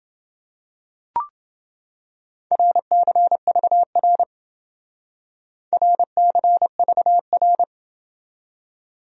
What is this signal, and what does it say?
Morse code, heard as a keyed tone, about 29 wpm, 700 Hz. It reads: RCVR RCVR